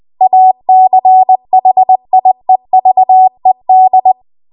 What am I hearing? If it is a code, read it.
Morse code, heard as a keyed tone, about 20 wpm, 750 Hz. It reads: ACHIEVED
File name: Archive / 20 WPM / 20WPM